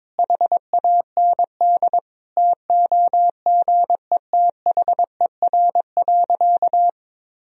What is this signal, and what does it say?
Morse code, heard as a keyed tone, about 22 wpm, 700 Hz. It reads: HAND TOGETHER.
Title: Sentences from Top 1000 Words - Review Part 4 - 22wpm